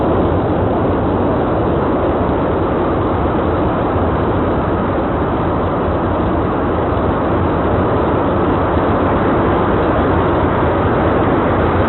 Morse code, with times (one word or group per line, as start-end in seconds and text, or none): none